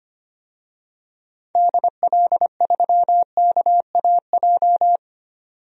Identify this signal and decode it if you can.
Morse code, heard as a keyed tone, about 25 wpm, 700 Hz. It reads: DL3KAJ
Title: DXCC-100 Call Signs - 6 Characters - 25wpm